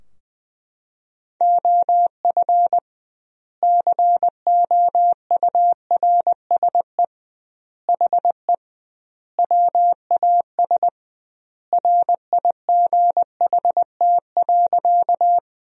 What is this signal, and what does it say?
Morse code, heard as a keyed tone, about 21 wpm, 700 Hz. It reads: OF COURSE HE WAS RIGHT.
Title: 2x Word-Spacing - Sentences from Top 500 Words - 20wpm